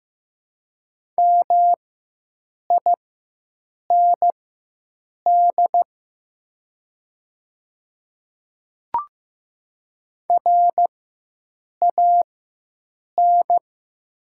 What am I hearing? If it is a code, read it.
Morse code, heard as a keyed tone, about 15 wpm, spaced as for 7 wpm, 690 Hz. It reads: MIND RAN
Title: 4x Character Spacing - Top 500 Words - 15wpm